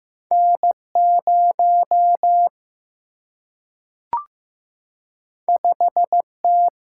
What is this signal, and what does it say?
Morse code, heard as a keyed tone, about 15 wpm, 700 Hz. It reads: N 0 5 T
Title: Single-Letter Single-Number 15wpm